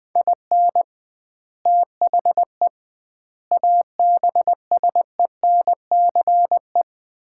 Morse code, heard as a keyed tone, about 21 wpm, 700 Hz. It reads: IN THE ABSENCE
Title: 2x Word-Spacing - Sets of 4 Words - 20wpm